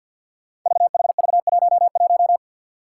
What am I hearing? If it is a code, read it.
Morse code, heard as a keyed tone, about 52 wpm, 700 Hz. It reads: VHV11
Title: Five Challenging Letter-Number Combinations 50wpm